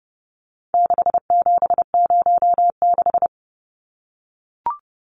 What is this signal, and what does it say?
Morse code, heard as a keyed tone, about 30 wpm, 700 Hz. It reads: 6706
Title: Numbers - 4-Digit 30wpm